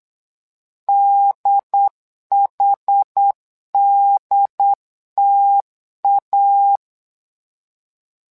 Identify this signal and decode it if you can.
Morse code, heard as a keyed tone, about 8 wpm, 800 Hz. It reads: DHDTA
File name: LEZIONI / 2 UHDN / 2 UHD 8wpm 2258